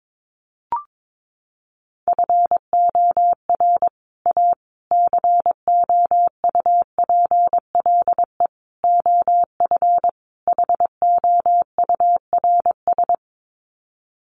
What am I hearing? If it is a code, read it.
Morse code, heard as a keyed tone, about 22 wpm, 700 Hz. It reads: FOR A COUPLE OF HOURS
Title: Sets of 5 Words Encore2 22wpm